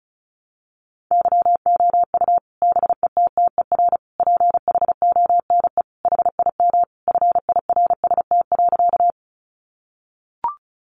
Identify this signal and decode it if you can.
Morse code, heard as a keyed tone, about 34 wpm, 700 Hz. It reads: YOU BETTER PHONE HIM FIRST.
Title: Sentences from Top 1100 Words 35wpm